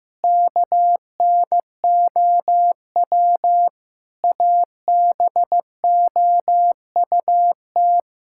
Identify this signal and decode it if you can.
Morse code, heard as a keyed tone, about 15 wpm, 700 Hz. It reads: KNOW ABOUT
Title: Sentences from Top 1000 Words - Review Part 6 - 15wpm